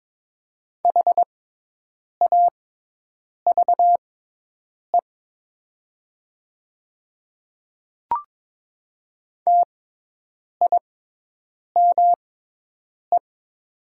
Morse code, heard as a keyed tone, about 22 wpm, spaced as for 8 wpm, 700 Hz. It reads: HAVE TIME
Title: 6x Character Spacing - Top 100 Words - 22wpm